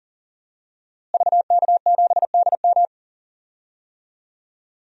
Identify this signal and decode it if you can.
Morse code, heard as a keyed tone, about 40 wpm, 700 Hz. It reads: VK7DM
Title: DXCC-100 Call Signs - 5 Characters - Repeated 2x - 40wpm